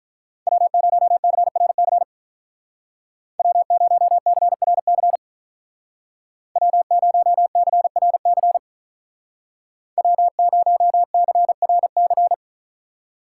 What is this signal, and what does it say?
Morse code, heard as a keyed tone, about 44 wpm, 700 Hz. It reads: W0CRC W0CRC W0CRC W0CRC